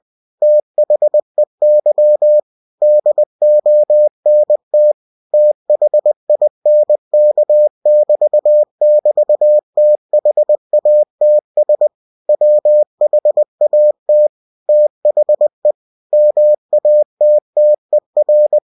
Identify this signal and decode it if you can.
Morse code, heard as a keyed tone, about 20 wpm, 600 Hz. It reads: THEY DONT THINK <BT> <BT> THATS WHAT THE MATTER
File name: cw-books / winnie-the-pooh / winnie-20wpm-page-0088